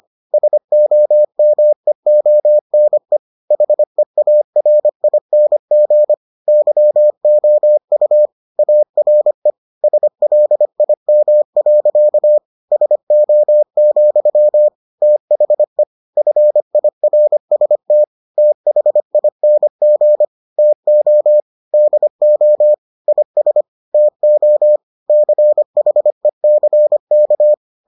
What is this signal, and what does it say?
Morse code, heard as a keyed tone, about 25 wpm, 600 Hz. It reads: SOMEONE HEARING YOU ARE SLIM. SO, THE FIRST THING TO DO IS TO CHECK